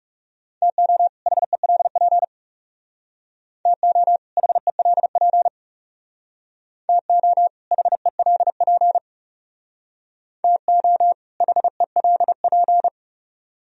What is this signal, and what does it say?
Morse code, heard as a keyed tone, about 37 wpm, 700 Hz. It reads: TO HELP TO HELP TO HELP TO HELP
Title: Speed-Racing - Sets of 2 Words 30wpm